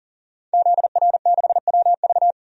Morse code, heard as a keyed tone, about 38 wpm, 700 Hz. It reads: ZR6WV
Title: DXCC-100 Call Signs - 5 Characters - 40wpm